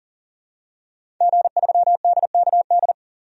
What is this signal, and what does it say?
Morse code, heard as a keyed tone, about 40 wpm, 700 Hz. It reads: G3DKD
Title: DXCC-100 Call Signs - 5 Characters - 40wpm